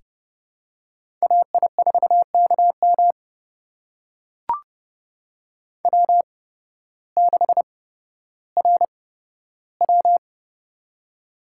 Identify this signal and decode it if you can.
Morse code, heard as a keyed tone, about 31 wpm, spaced as for 14 wpm, 700 Hz. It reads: AI4KM W6RW
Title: USA Call Signs - 8x Character Spacing - 30wpm